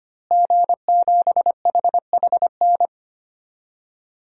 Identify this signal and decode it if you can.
Morse code, heard as a keyed tone, about 25 wpm, 700 Hz. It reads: G7HHN
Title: DXCC-100 Call Signs - 5 Characters - Repeated 2x - 25wpm